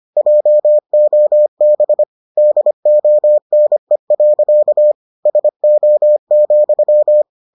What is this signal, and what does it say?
Morse code, heard as a keyed tone, about 25 wpm, 600 Hz. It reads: JOB DONE. SO,